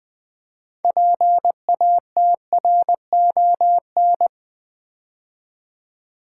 Morse code, heard as a keyed tone, about 21 wpm, 700 Hz. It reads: PATRON
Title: 6 Letter Words - First 3 Letters Given - 20wpm